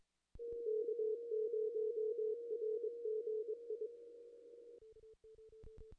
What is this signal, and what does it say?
Morse code, heard as a keyed tone, about 24 wpm, 430 Hz. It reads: K0RGI
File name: contests / ss-cw-k5tr-op-n5rz -2300